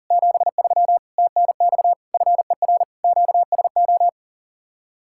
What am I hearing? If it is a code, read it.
Morse code, heard as a keyed tone, about 50 wpm, 700 Hz. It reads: 73 TNX FER QSO
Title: Speed-Racing Common QSO Phrases - 40wpm